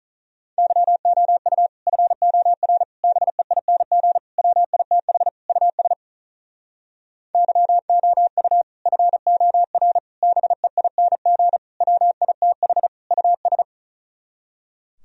Morse code, heard as a keyed tone, about 44 wpm, 700 Hz. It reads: YOU FOR BEING WITH US YOU FOR BEING WITH US